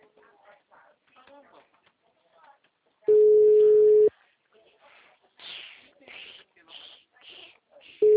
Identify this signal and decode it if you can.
Ringback tone (Europe, 425 Hz), with speech.